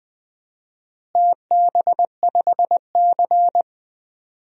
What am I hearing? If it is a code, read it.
Morse code, heard as a keyed tone, about 21 wpm, 700 Hz. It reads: TB5C
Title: DXCC-100 Call Signs - 4 Characters - Repeated 2x - 20wpm